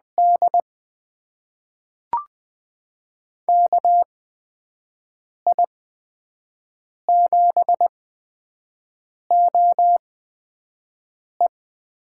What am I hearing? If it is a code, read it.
Morse code, heard as a keyed tone, about 20 wpm, spaced as for 5 wpm, 700 Hz. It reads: D KI7OE
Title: USA Call Signs - 8x Character Spacing - 20wpm